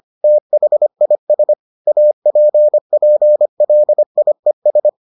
Morse code, heard as a keyed tone, about 25 wpm, 600 Hz. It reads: THIS APPLIES